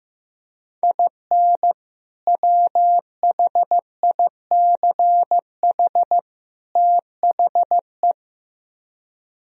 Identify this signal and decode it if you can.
Morse code, heard as a keyed tone, about 15 wpm, 700 Hz. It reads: IN WHICH THE
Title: Speed-Racing - Sets of 3 Words 15wpm